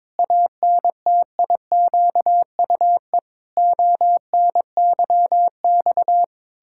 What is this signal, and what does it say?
Morse code, heard as a keyed tone, about 21 wpm, 700 Hz. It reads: ANTIQUE ONYX
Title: Pangram Challenge v2 - 22wpm